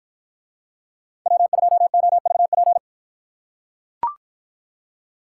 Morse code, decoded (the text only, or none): W2OVP